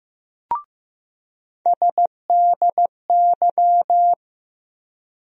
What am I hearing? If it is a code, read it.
Morse code, heard as a keyed tone, about 16 wpm, 700 Hz. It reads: SDY
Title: USA Call Sign Postfixes - 3 Letters 15wpm